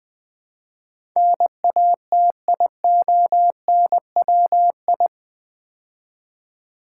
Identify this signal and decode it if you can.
Morse code, heard as a keyed tone, about 20 wpm, 700 Hz. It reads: NATIONWI
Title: Build A Word - 4 Letters At A Time - 10-Letter Words - 20wpm